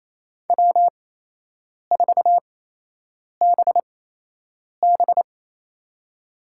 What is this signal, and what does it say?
Morse code, heard as a keyed tone, about 28 wpm, spaced as for 8 wpm, 700 Hz. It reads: W4BB
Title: USA Call Signs - 8x Character Spacing - 28wpm